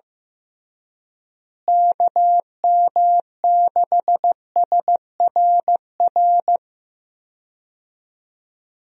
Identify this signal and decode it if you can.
Morse code, heard as a keyed tone, about 15 wpm, 700 Hz. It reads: KM6SRR
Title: USA Call Signs - 6 Characters - Repeated 3x - 15wpm